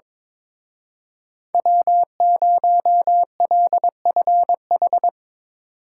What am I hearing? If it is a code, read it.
Morse code, heard as a keyed tone, about 24 wpm, 700 Hz. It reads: W0LFH
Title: Speed Racing - USA Call Signs 22wpm